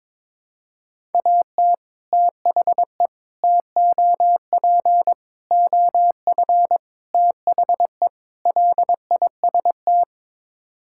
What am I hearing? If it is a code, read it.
Morse code, heard as a keyed tone, about 22 wpm, 700 Hz. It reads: AT THE TOP OF THE LIST
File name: practice / 22wpm / Sets of 6 Words - 22wpm